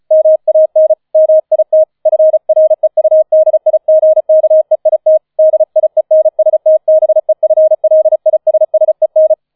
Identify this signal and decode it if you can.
Morse code, heard as a keyed tone, about 34 wpm, 630 Hz. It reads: MAN MIT FREUDIGKEIT DIENSTBEFLISSEN
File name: 005 Fünfter Streich 35wpm